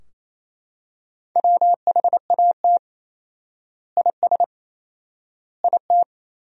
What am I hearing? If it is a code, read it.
Morse code, heard as a keyed tone, about 28 wpm, 700 Hz. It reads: WHAT IS IT
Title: 4x Word-Spacing - Sentences from Top 100 Words - 28wpm